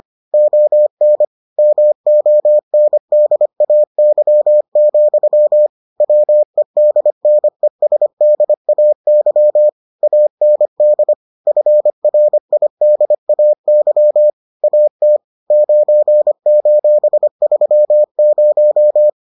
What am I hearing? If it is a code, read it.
Morse code, heard as a keyed tone, about 25 wpm, 600 Hz. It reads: ON MONDAY, WEDNESDAY AND FRIDAY AT 9:30